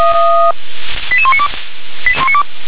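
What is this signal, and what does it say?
Five-tone selective call (ZVEI), heard as quick runs of five tones, tones of about 70 ms. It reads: 88192
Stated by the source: ZVEI five-tone selective call received on SDR.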